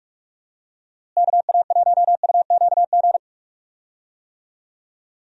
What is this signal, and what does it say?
Morse code, heard as a keyed tone, about 46 wpm, 700 Hz. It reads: KA1UQG
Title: USA Call Signs - 6 Characters - Repeated 3x - 45wpm